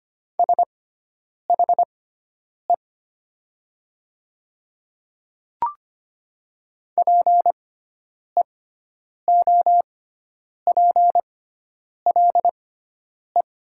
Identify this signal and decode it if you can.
Morse code, heard as a keyed tone, about 25 wpm, spaced as for 9 wpm, 700 Hz. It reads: SHE PEOPLE
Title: 6x Character Spacing - Top 100 Words - 25wpm